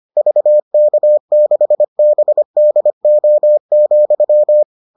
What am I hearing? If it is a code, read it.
Morse code, heard as a keyed tone, about 25 wpm, 600 Hz. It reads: VK6BDO,